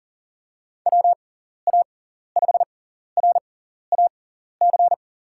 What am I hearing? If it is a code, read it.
Morse code, heard as a keyed tone, about 39 wpm, spaced as for 14 wpm, 700 Hz. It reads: WA5RAC
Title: USA Call Signs - 6 Characters - 6x Character Spacing - 40wpm